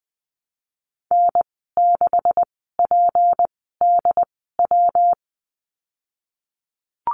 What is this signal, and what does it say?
Morse code, heard as a keyed tone, about 20 wpm, spaced as for 15 wpm, 700 Hz. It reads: N6PDW
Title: 161 Lesson 26g - Letter g - Call Signs 20wpm